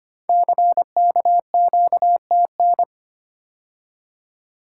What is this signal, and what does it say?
Morse code, heard as a keyed tone, about 25 wpm, 690 Hz. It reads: CKQTN
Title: Code Groups - 5 Letters - 25wpm